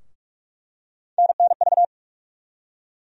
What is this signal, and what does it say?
Morse code, heard as a keyed tone, about 44 wpm, 700 Hz. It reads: NNV
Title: USA Call Sign Postfixes - 3 Letters 45wpm